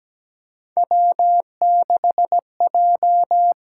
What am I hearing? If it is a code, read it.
Morse code, heard as a keyed tone, about 17 wpm, 700 Hz. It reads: W6J
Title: USA Call Signs - 8x Character Spacing - 17wpm